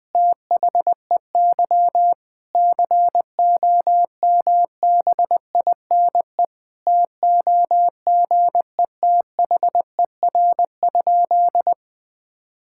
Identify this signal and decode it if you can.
Morse code, heard as a keyed tone, about 21 wpm, 700 Hz. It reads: THEY COMBINE TOGETHER?
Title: Sentences from Top 1600 Words 20wpm